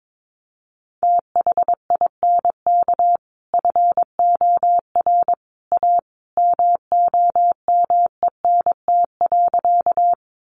Morse code, heard as a keyed tone, about 22 wpm, 700 Hz. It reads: THINK FOR A MOMENT.
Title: Sentences from Top 1000 Words 22wpm